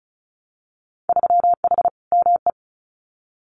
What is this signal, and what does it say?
Morse code, heard as a keyed tone, about 36 wpm, 700 Hz. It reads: 3H ME
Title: Winter Field Day Exchanges Encore 35wpm